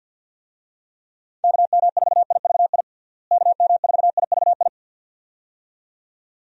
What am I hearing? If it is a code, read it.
Morse code, heard as a keyed tone, about 50 wpm, 700 Hz. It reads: KM4IVI KM4IVI
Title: USA Call Signs - 6 Characters - Repeated 2x - 50wpm